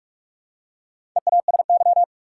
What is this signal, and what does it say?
Morse code, heard as a keyed tone, about 44 wpm, 700 Hz. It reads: EASY
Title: Four Letter Words - 45wpm